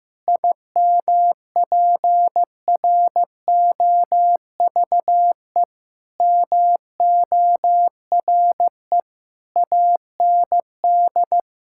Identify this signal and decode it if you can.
Morse code, heard as a keyed tone, about 15 wpm, 700 Hz. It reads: IMPROVE MORE AND
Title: Sentences from Top 1000 Words - Review Part 5 - 15wpm